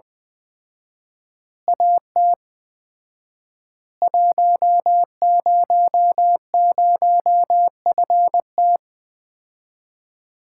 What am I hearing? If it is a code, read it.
Morse code, heard as a keyed tone, about 20 wpm, 700 Hz. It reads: AT 100FT
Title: Common QSO Phrases - 4x Word-Spacing - 20wpm